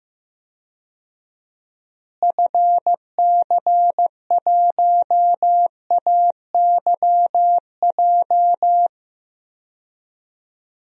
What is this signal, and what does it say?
Morse code, heard as a keyed tone, about 15 wpm, 700 Hz. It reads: FC1AYJ
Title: DXCC-100 Call Signs - 6 Characters - Repeated 3x - 15wpm